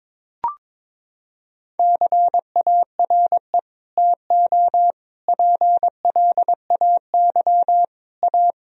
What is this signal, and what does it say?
Morse code, heard as a keyed tone, about 22 wpm, 700 Hz. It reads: CARE TO PLAY A
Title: Sentences from Top 1000 Words - Review Part 1 - 22wpm